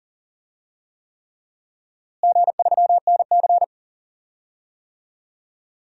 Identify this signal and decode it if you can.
Morse code, heard as a keyed tone, about 40 wpm, 700 Hz. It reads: G3NC
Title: DXCC-100 Call Signs - 4 Characters - Repeated 3x - 40wpm